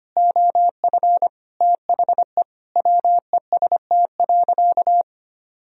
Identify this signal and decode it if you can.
Morse code, heard as a keyed tone, about 25 wpm, 700 Hz. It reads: OF THE WEST.
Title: Sentences from Top 1000 Words - Review Part 2 - 25wpm